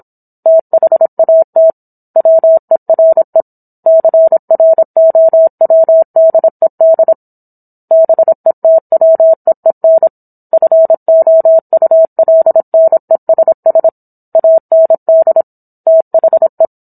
Morse code, heard as a keyed tone, about 27 wpm, 650 Hz. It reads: THAT WERE CROWDED BETWEEN FOULNESS AND THE